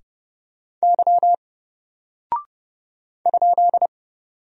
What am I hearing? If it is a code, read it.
Morse code, heard as a keyed tone, about 30 wpm, 700 Hz. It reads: Y ?
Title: Single Character (Letter:Number:Punctuation:Prosign) Alt 30wpm